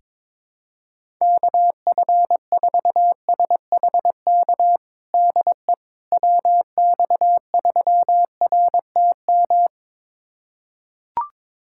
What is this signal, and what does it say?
Morse code, heard as a keyed tone, about 22 wpm, 700 Hz. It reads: KF4SHK DE WX3RTM
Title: USA Callsign DE Callsign - 6 Characters - 22wpm